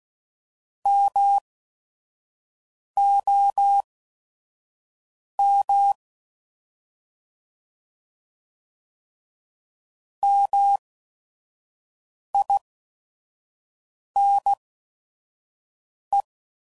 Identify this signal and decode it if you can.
Morse code, heard as a keyed tone, about 16 wpm, spaced as for 5 wpm, 800 Hz. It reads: MOM MINE